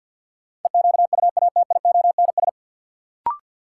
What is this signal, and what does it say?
Morse code, heard as a keyed tone, about 52 wpm, 700 Hz. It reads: EQUATIONS
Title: Sets of 5 Words - New Word Drill - 50wpm